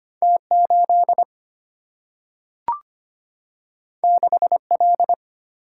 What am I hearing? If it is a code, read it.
Morse code, heard as a keyed tone, about 25 wpm, 700 Hz. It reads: T 8 6 L
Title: Single-Letter Single-Number 25wpm